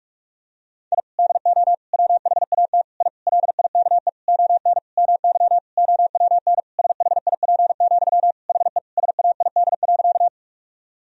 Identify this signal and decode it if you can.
Morse code, heard as a keyed tone, about 46 wpm, 700 Hz. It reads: I DO WHAT I LIKE ON MY OWN SHIP, HE SAID.
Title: Sentences from Top 1000 Words - Review Part 2 - 45wpm